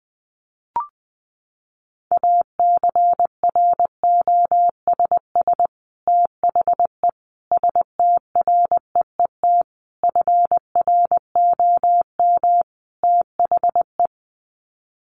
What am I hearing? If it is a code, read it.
Morse code, heard as a keyed tone, about 20 wpm, 700 Hz. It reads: ACROSS THE STREET FROM THE
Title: Sets of 5 Words Encore 20wpm